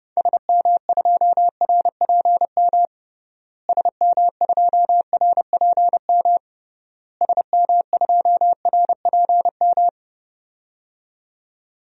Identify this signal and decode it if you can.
Morse code, heard as a keyed tone, about 32 wpm, 700 Hz. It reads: SM2RPM SM2RPM SM2RPM
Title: DXCC-100 Call Signs - 6 Characters - Repeated 3x - 30wpm